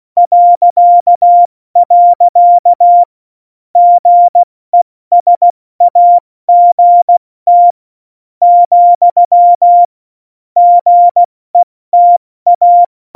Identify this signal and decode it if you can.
Morse code, heard as a keyed tone, about 16 wpm, spaced as for 14 wpm, 700 Hz. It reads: .. GESAGT , GETA
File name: Morsekurs / Witze / Witz 13 March 2018, 70 F80